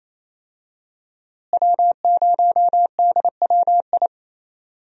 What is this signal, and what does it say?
Morse code, heard as a keyed tone, about 28 wpm, 700 Hz. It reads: W0DWI
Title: USA Call Signs - 5 Characters - Repeated 2x - 28wpm